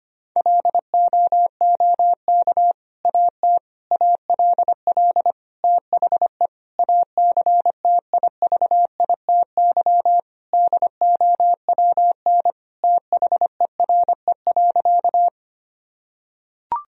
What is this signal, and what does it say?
Morse code, heard as a keyed tone, about 25 wpm, 700 Hz. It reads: LOOK AT ALL THE ACTIVITY DOWN THERE.
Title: Sentences from Top 1000 Words - Review Part 4 - 25wpm